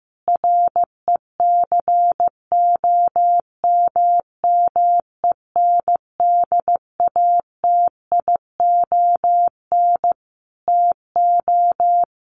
Morse code, heard as a keyed tone, about 14 wpm, 700 Hz. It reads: RECOMMENDATION TO
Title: Sentences from Top 1400 Words 15wpm